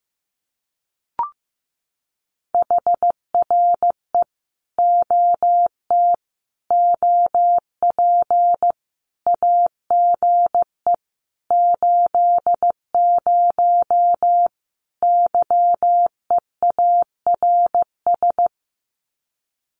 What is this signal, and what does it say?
Morse code, heard as a keyed tone, about 15 wpm, 700 Hz. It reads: HRE OT OP AGE 80 YEARS
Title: Common QSO Phrases 15wpm